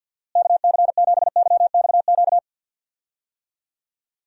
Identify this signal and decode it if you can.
Morse code, heard as a keyed tone, about 53 wpm, 700 Hz. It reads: KC7YXQ
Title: USA Call Signs - 6 Characters - Repeated 3x - 50wpm